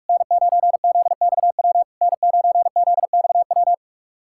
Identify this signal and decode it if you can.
Morse code, heard as a keyed tone, about 44 wpm, 690 Hz. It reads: N9ZXW N9ZXW
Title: SOTA QSO Practice 45wpm